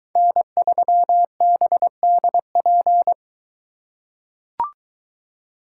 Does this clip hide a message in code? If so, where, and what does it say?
Yes: Morse code from 0.2 to 3.1 s, heard as a keyed tone, about 22 wpm, 700 Hz. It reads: N3BDP